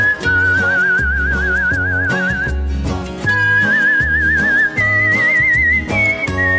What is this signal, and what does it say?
Emergency Alert System (SAME) tones (header bursts), heard as harsh buzzing bursts. It is heard from 0.3 to 5.8 s.